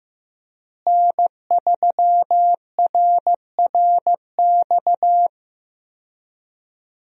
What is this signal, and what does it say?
Morse code, heard as a keyed tone, about 15 wpm, 700 Hz. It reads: N3RRX
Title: USA Call Signs - 5 Characters - Repeated 2x - 15wpm